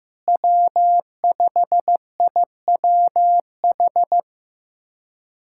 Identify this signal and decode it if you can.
Morse code, heard as a keyed tone, about 15 wpm, 700 Hz. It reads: W5IWH
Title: USA Callsign DE Callsign - 5 Characters - 15wpm